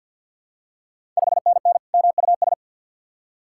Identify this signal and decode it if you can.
Morse code, heard as a keyed tone, about 51 wpm, 700 Hz. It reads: 5NN MUS
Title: Ontario QSO Party Exchange 50wpm